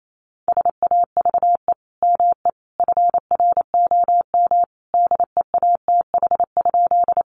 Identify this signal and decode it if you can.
Morse code, heard as a keyed tone, about 28 wpm, 700 Hz. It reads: SAVE ME FROM DEATH?
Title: Sentences from Top 1100 Words 28wpm